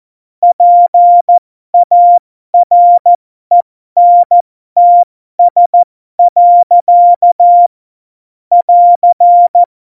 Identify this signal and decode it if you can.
Morse code, heard as a keyed tone, about 14 wpm, spaced as for 12 wpm, 700 Hz. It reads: PARENTS. <AR>
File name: Morsekurs / Witze